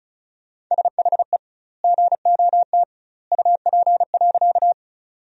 Morse code, heard as a keyed tone, about 36 wpm, 700 Hz. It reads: SHE GOT UP.
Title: 2x Word-Spacing - Sentences from Top 300 Words - 35wpm